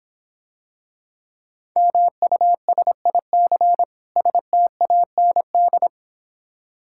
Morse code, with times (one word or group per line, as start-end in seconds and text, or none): 1.8-3.8 MUSIC
4.2-5.9 STAND